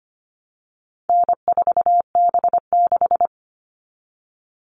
Morse code, heard as a keyed tone, about 25 wpm, 700 Hz. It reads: N4B6
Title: All Cut Numbers - 4-Digit 25wpm